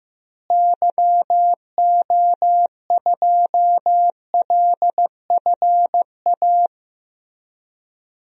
Morse code, heard as a keyed tone, about 15 wpm, 700 Hz. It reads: YO2LFA
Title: DXCC-100 Call Signs - 6 Characters - Repeated 2x - 15wpm